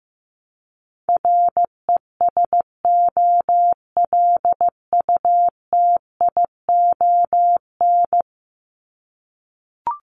Morse code, heard as a keyed tone, about 15 wpm, 700 Hz. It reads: RESOLUTION
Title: Top 1501-1750 Words 15wpm